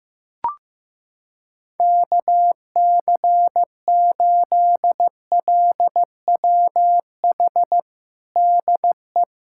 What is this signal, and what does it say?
Morse code, heard as a keyed tone, about 15 wpm, 700 Hz. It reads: KC8LWH DE
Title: USA Callsign DE Callsign 15wpm